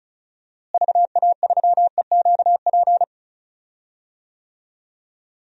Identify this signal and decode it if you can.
Morse code, heard as a keyed tone, about 36 wpm, 700 Hz. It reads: VA3EQP
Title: DXCC-100 Call Signs - 6 Characters - Repeated 3x - 35wpm